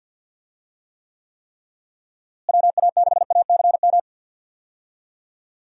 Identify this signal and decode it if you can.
Morse code, heard as a keyed tone, about 50 wpm, 700 Hz. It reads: WA6ACM
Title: USA Call Signs - 6 Characters - Repeated 3x - 50wpm